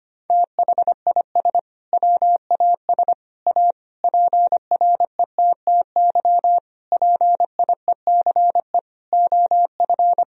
Morse code, heard as a keyed tone, about 25 wpm, 700 Hz. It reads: THIS WAS A PRETTY PIECE OF
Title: Sentences from Top 1000 Words - Review Part 2 - 25wpm